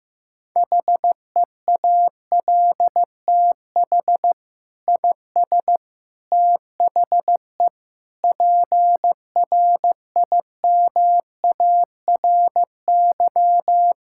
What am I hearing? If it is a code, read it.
Morse code, heard as a keyed tone, about 16 wpm, 700 Hz. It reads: HEALTH IS THE PRIMARY
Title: Sentences from Top 1000 Words - Review Part 5 - 15wpm